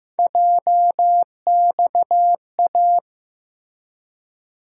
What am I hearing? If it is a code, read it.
Morse code, heard as a keyed tone, about 15 wpm, 700 Hz. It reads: JXA
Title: USA Call Sign Postfixes - 3 Letters 15wpm